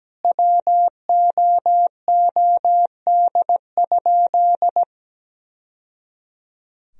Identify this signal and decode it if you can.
Morse code, heard as a keyed tone, about 17 wpm, 700 Hz. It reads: WOOD?
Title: Sentences from Top 1000 Words - Review Part 2 - 17wpm